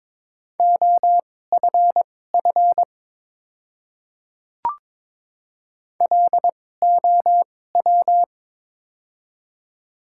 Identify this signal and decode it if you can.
Morse code, heard as a keyed tone, about 22 wpm, spaced as for 16 wpm, 700 Hz. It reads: OFF LOW
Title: 2x Character Spacing - Top 200 Words - 22wpm